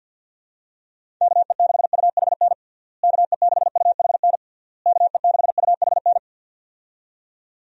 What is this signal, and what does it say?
Morse code, heard as a keyed tone, about 50 wpm, 700 Hz. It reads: KE6UHN KE6UHN KE6UHN
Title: USA Call Signs - 6 Characters - Repeated 3x - 50wpm